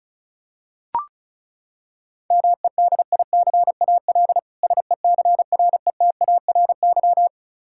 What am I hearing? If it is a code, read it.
Morse code, heard as a keyed tone, about 36 wpm, 700 Hz. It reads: MEDICAL SECRETARY
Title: K1USN SST QSO Practice 35wpm